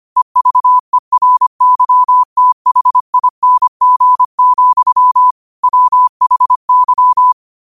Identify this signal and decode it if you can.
Morse code, heard as a keyed tone, about 25 wpm, 1000 Hz. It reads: EVERYTHING, WHY